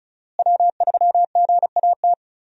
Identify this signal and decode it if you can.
Morse code, heard as a keyed tone, about 36 wpm, 700 Hz. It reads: W3GAT
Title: USA Call Signs - 5 Characters - Repeated 3x - 35wpm